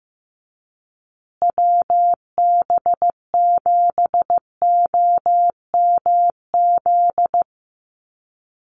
Morse code, heard as a keyed tone, about 15 wpm, 700 Hz. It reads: WB7OMZ
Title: USA Call Signs Repeated 2x 15wpm